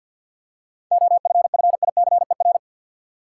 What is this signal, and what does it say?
Morse code, heard as a keyed tone, about 53 wpm, 700 Hz. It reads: OFFICER
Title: Seven Letter Words - 50wpm